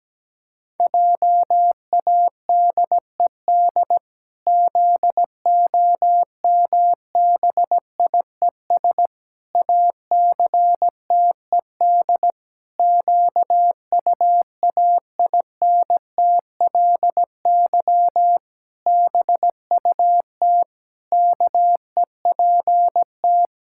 Morse code, heard as a keyed tone, about 17 wpm, 700 Hz. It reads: JADED ZOMBIES ACTED QUAINTLY BUT KEPT
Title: Pangram Challenge v2 - 17wpm